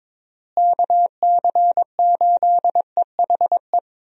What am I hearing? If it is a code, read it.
Morse code, heard as a keyed tone, about 22 wpm, 700 Hz. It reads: KC8EHE
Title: USA Call Signs - 6 Characters - Repeated 3x - 22wpm